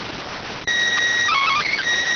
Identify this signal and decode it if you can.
SSTV (slow-scan television), PD 290.